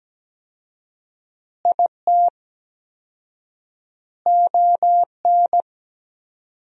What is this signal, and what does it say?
Morse code, heard as a keyed tone, about 17 wpm, 700 Hz. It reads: IT ON
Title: 4x Word-Spacing - Sentences from Top 400 Words - 17wpm